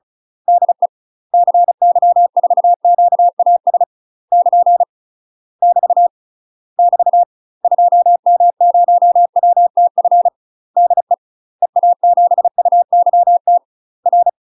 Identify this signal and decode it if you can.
Morse code, heard as a keyed tone, about 35 wpm, 700 Hz. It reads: DE CY4QAS <KN> <BT> <BT> 2M0WTF DE EA7UYT R
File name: UK MASTER QSOs TRAINING / 35WPM / UK MASTER QSOs 0042